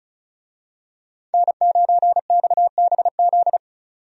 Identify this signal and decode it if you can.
Morse code, heard as a keyed tone, about 36 wpm, 700 Hz. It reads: N9XBZ
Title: USA Call Signs - 5 Characters - 35wpm